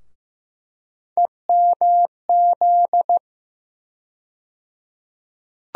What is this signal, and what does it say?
Morse code, heard as a keyed tone, about 15 wpm, 700 Hz. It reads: EMZ
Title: USA Call Sign Postfixes - 3 Letters 15wpm